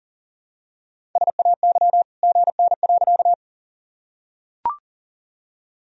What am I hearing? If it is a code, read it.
Morse code, heard as a keyed tone, about 40 wpm, 700 Hz. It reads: SAY GN.
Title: Sentences from Top 300 Words and Abbreviations 40wpm